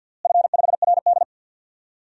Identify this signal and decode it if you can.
Morse code, heard as a keyed tone, about 46 wpm, 700 Hz. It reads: F5RD
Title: DXCC-100 Call Signs - 4 Characters - Repeated 2x - 50wpm